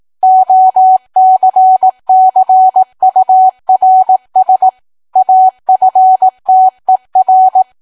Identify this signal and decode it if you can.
Morse code, heard as a keyed tone, about 18 wpm, 750 Hz. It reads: OCCURS AFTER